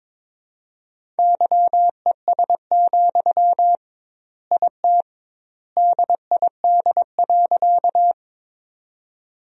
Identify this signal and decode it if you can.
Morse code, heard as a keyed tone, about 22 wpm, 700 Hz. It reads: YES, IT DID.
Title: 2x Word-Spacing - Sentences from Top 500 Words - 22wpm